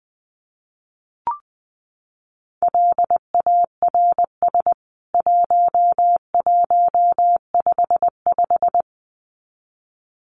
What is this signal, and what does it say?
Morse code, heard as a keyed tone, about 19 wpm, 700 Hz. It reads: LARS 1155
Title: CWT Exchange 20wpm